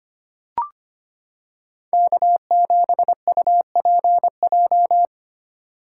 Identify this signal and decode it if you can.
Morse code, heard as a keyed tone, about 25 wpm, 700 Hz. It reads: K7UPJ
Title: K1USN SST QSO Practice - 25wpm